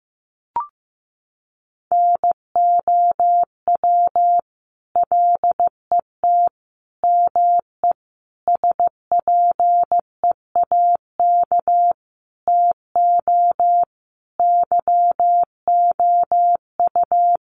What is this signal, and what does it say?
Morse code, heard as a keyed tone, about 15 wpm, 700 Hz. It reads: NOW LET ME SPEAK TO YOU
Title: Sentences from Top 1200 Words 15wpm